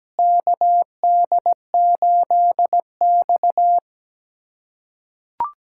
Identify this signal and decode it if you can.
Morse code, heard as a keyed tone, about 17 wpm, 700 Hz. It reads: KD8X
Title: USA Callsign DE Callsign - 4 Characters - 17wpm